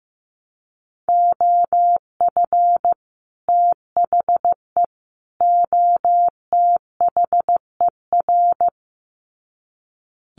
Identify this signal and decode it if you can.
Morse code, heard as a keyed tone, about 15 wpm, 700 Hz. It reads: OF THE OTHER
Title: Sets of 3 Words 15wpm